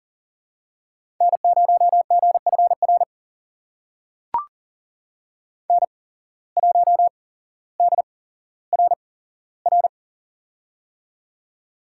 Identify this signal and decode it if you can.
Morse code, heard as a keyed tone, about 40 wpm, spaced as for 17 wpm, 700 Hz. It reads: N0GFR N1DRR
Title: USA Call Signs - 5 Characters - 8x Character Spacing - 40wpm